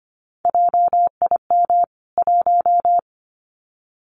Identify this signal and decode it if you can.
Morse code, heard as a keyed tone, about 25 wpm, 700 Hz. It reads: JIM 1
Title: CWT Exchange 25wpm